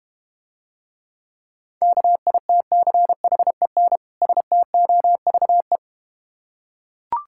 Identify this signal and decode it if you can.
Morse code, heard as a keyed tone, about 32 wpm, 700 Hz. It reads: KITCHEN STOVE